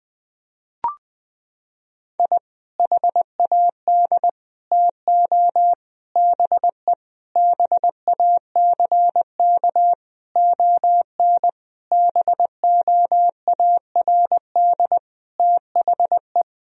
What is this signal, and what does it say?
Morse code, heard as a keyed tone, about 20 wpm, 700 Hz. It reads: I HAD TO BE BACK ON BOARD THE
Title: Sentences from Top 1000 Words - Review Part 4 - 20wpm